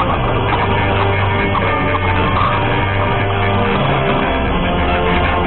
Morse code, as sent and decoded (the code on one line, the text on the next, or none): none